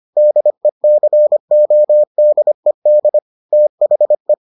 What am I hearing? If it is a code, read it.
Morse code, heard as a keyed tone, about 25 wpm, 600 Hz. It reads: DECODED THE